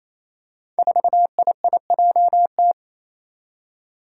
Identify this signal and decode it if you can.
Morse code, heard as a keyed tone, about 27 wpm, 700 Hz. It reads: 4IIJT
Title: Five Challenging Letter-Number Combinations 28wpm